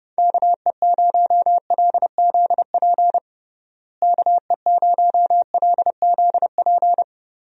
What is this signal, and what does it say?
Morse code, heard as a keyed tone, about 30 wpm, 700 Hz. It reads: KE0LZP KE0LZP
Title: USA Call Signs - 6 Characters - Repeated 3x - 30wpm